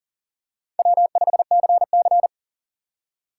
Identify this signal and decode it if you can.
Morse code, heard as a keyed tone, about 38 wpm, 700 Hz. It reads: W5CC
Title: USA Call Signs - 4 Characters - 4x Character Spacing - 40wpm